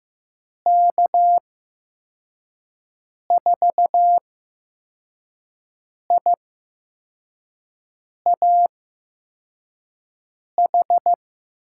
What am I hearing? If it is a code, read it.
Morse code, heard as a keyed tone, about 15 wpm, spaced as for 4 wpm, 700 Hz. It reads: K4IAH
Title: USA Call Signs - 8x Character Spacing - 15wpm